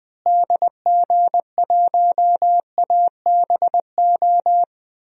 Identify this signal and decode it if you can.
Morse code, heard as a keyed tone, about 20 wpm, 700 Hz. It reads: DG1ABO
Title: DXCC-100 Call Signs - 6 Characters - Repeated 3x - 20wpm